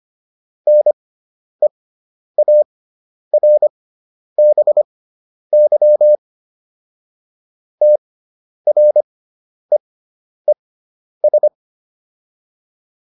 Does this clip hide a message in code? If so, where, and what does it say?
Yes: Morse code from 0.7 to 11.5 s, heard as a keyed tone, about 25 wpm, spaced as for 10 wpm, 600 Hz. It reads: NEARBY TREES